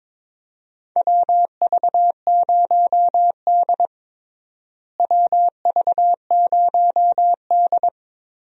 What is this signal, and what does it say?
Morse code, heard as a keyed tone, about 22 wpm, 700 Hz. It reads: WV0D WV0D
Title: USA Call Signs - 4 Characters - Repeated 3x - 22wpm